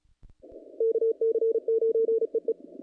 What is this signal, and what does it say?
Morse code, heard as a keyed tone, about 33 wpm, 450 Hz. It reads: KC9EE
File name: contests / naqp-cw-k5tr-op-k5pi- -2100